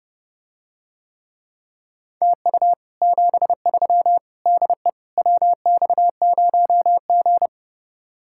Morse code, heard as a keyed tone, about 32 wpm, 700 Hz. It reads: TU 73 DE WX0G